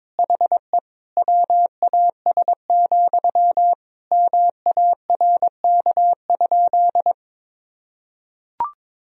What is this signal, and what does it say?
Morse code, heard as a keyed tone, about 22 wpm, 700 Hz. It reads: HE WAS, MARK?
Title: Sentences from Top 1000 Words - Review Part 1 - 22wpm